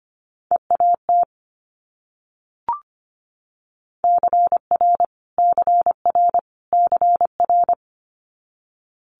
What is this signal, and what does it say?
Morse code, heard as a keyed tone, about 25 wpm, 700 Hz. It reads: EAT CR CR CR
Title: Multi-letter Phonemes Repeated 3x 25wpm